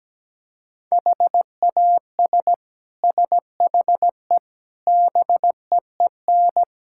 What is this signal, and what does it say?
Morse code, heard as a keyed tone, about 17 wpm, 700 Hz. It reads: HAS SHE BEEN
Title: Sentences from Top 1000 Words - Review Part 6 - 17wpm